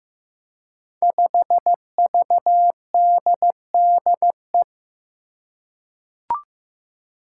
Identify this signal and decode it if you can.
Morse code, heard as a keyed tone, about 15 wpm, 700 Hz. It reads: 5VDDE
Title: Five Challenging Letter-Number Combinations 15wpm